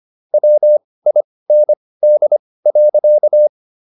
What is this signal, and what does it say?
Morse code, heard as a keyed tone, about 25 wpm, spaced as for 18 wpm, 600 Hz. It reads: WIND.